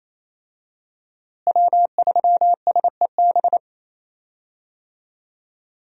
Morse code, heard as a keyed tone, about 27 wpm, 700 Hz. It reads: W3SEB
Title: USA Call Signs - 5 Characters - Repeated 2x - 28wpm